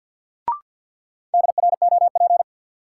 Morse code, heard as a keyed tone, about 53 wpm, 700 Hz. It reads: DROP
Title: Sentences from Top 1400 Words - New Word Drill - 50wpm